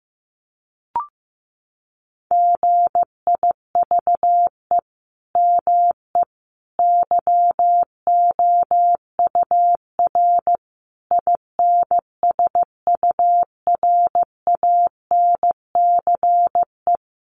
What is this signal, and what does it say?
Morse code, heard as a keyed tone, about 15 wpm, 700 Hz. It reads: GIVE ME YOUR INSURANCE
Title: Sentences from Top 1100 Words 15wpm